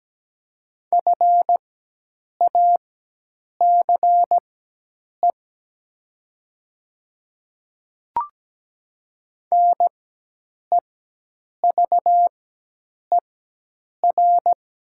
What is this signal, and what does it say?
Morse code, heard as a keyed tone, about 17 wpm, spaced as for 8 wpm, 700 Hz. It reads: FACE NEVER
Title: 4x Character Spacing - Top 300 Words - 17wpm